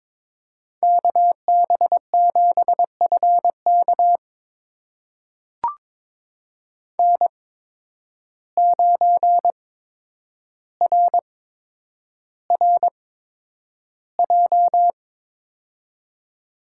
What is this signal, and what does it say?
Morse code, heard as a keyed tone, about 22 wpm, spaced as for 9 wpm, 700 Hz. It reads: KB7FK N9RRJ
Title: USA Call Signs - 5 Characters - 8x Character Spacing - 22wpm